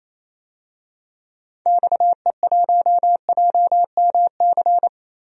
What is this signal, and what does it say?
Morse code, heard as a keyed tone, about 27 wpm, 700 Hz. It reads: XE1JMC
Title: DXCC-100 Call Signs - 6 Characters - Repeated 2x - 28wpm